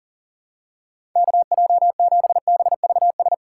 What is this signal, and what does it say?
Morse code, heard as a keyed tone, about 40 wpm, 700 Hz. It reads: KJ7BVS
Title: USA Call Signs - 6 Characters - 4x Character Spacing - 40wpm